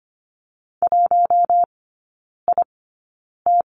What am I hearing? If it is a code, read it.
Morse code, heard as a keyed tone, about 25 wpm, spaced as for 9 wpm, 700 Hz. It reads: 1IT
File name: Lesson 17e - Letter u - Three Characters 25wpm